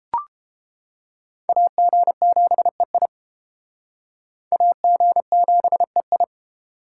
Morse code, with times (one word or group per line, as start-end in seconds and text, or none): 1.5-3.1 AG7EI
4.5-6.2 AG7EI